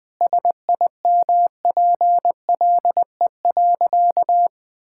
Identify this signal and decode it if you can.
Morse code, heard as a keyed tone, about 20 wpm, 700 Hz. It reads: SIMPLE.
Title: Sentences from Top 1000 Words - Review Part 5 - 20wpm